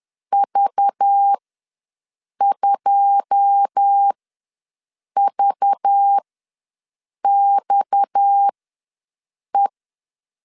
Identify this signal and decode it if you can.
Morse code, heard as a keyed tone, about 10 wpm, spaced as for 6 wpm, 800 Hz. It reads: V2VXE
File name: LEZIONI / 9 2 ws 10wpm 4130